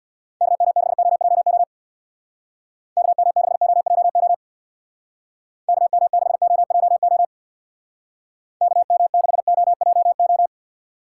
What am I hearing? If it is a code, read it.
Morse code, heard as a keyed tone, about 62 wpm, 700 Hz. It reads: KM6OJO KM6OJO KM6OJO KM6OJO
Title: Speed Racing - USA 2x3 Call Signs 50wpm